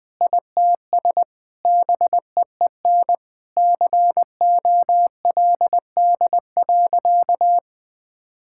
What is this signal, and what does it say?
Morse code, heard as a keyed tone, about 20 wpm, 700 Hz. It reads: ITS BEEN COLD.
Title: Sentences with 3-Word Phrases - 20wpm